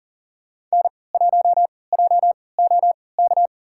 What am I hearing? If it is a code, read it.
Morse code, heard as a keyed tone, about 40 wpm, spaced as for 23 wpm, 700 Hz. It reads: N1JOK
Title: USA Call Signs - 5 Characters - 3x Character Spacing - 40wpm